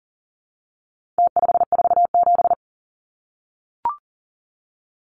Morse code, heard as a keyed tone, about 40 wpm, 700 Hz. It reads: T547
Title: Numbers - Contest Style - 4-Digit 40wpm